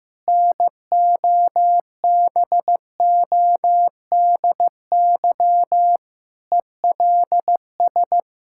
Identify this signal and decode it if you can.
Morse code, heard as a keyed tone, about 16 wpm, 700 Hz. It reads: NOBODY ELS
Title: Sentences from Top 1600 Words 15wpm